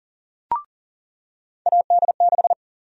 Morse code, heard as a keyed tone, about 39 wpm, 700 Hz. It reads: AD6
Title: USA Call Sign Prefixes - 40wpm